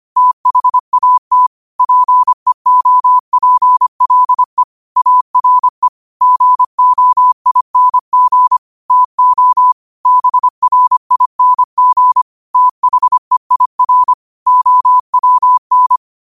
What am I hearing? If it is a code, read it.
Morse code, heard as a keyed tone, about 25 wpm, 1000 Hz. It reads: THAT PEOPLE ARE GOING TO BRING THEIR OWN
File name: qotdfiles / 25wpm